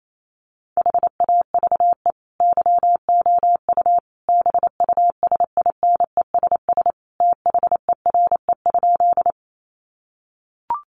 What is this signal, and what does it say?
Morse code, heard as a keyed tone, about 28 wpm, 700 Hz. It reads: HAVE YOU BUSINESS THERE?
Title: Sentences from Top 600 Words 28wpm